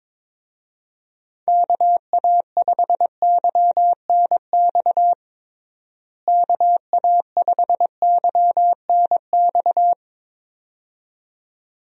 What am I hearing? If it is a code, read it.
Morse code, heard as a keyed tone, about 22 wpm, 700 Hz. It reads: KA5YNX KA5YNX
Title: USA Call Signs - 6 Characters - Repeated 2x - 22wpm